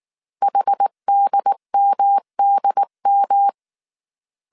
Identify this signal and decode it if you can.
Morse code, heard as a keyed tone, about 19 wpm, 800 Hz. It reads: HDMDM
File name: CW / DET / 1 3 1500C S B 18wpm 1916